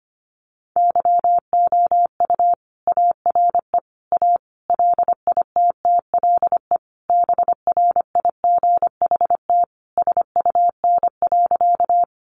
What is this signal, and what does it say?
Morse code, heard as a keyed tone, about 25 wpm, 700 Hz. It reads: YOU ARE A LITTLE BRIGHT SUN.
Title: Sentences from Top 1000 Words 25wpm